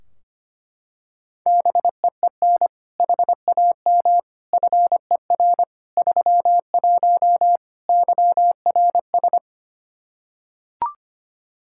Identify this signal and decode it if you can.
Morse code, heard as a keyed tone, about 25 wpm, 700 Hz. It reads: BEEN HAM FER 31 YRS
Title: K1USN SST QSO Practice 25wpm